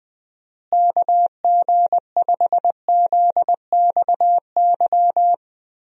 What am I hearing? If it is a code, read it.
Morse code, heard as a keyed tone, about 19 wpm, 700 Hz. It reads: KG5ZXY
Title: USA Call Signs - 6 Characters - Repeated 2x - 20wpm